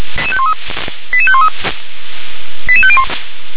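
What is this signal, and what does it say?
Five-tone selective call (ZVEI), heard as quick runs of five tones, tones of about 70 ms. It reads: 88513 88501
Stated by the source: ZVEI five-tone call received by SDR.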